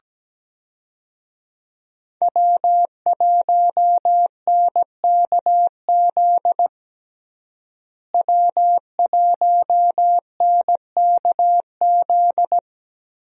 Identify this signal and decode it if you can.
Morse code, heard as a keyed tone, about 17 wpm, 700 Hz. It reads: W1NKZ W1NKZ
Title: USA Call Signs - 5 Characters - Repeated 3x - 17wpm